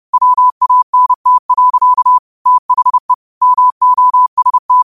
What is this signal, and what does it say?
Morse code, heard as a keyed tone, about 30 wpm, 1000 Hz. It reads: WANT. THE MOST